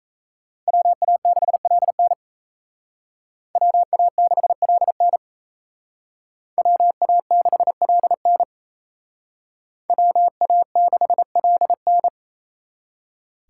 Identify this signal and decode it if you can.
Morse code, heard as a keyed tone, about 37 wpm, 700 Hz. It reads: WA6LN WA6LN WA6LN WA6LN